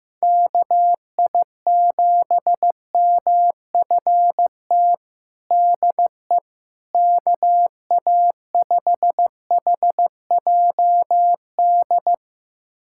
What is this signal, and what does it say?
Morse code, heard as a keyed tone, about 15 wpm, 700 Hz. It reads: KI7MFT DE KA5HJD
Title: USA Callsign DE Callsign - 6 Characters - 15wpm